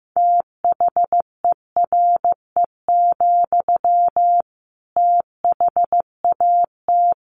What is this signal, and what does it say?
Morse code, heard as a keyed tone, about 15 wpm, 700 Hz. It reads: THERE, THAT
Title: Sentences from Top 200 Words 15wpm